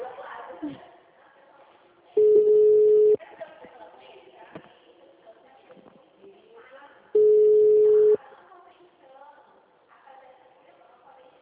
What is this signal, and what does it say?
Ringback tone (Europe, 425 Hz), with speech.